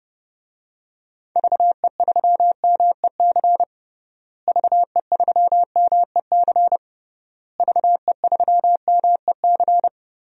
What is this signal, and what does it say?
Morse code, heard as a keyed tone, about 30 wpm, 700 Hz. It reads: VE3MEC VE3MEC VE3MEC
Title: DXCC-100 Call Signs - 6 Characters - Repeated 3x - 30wpm